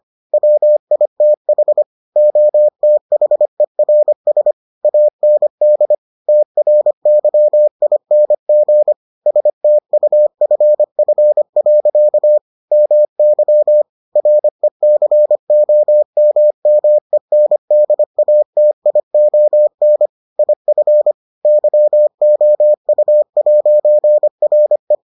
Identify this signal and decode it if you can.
Morse code, heard as a keyed tone, about 25 wpm, 600 Hz. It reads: WITH OTHERS AND TRYING STUFF. MY RECOMMENDATION IF YOU'RE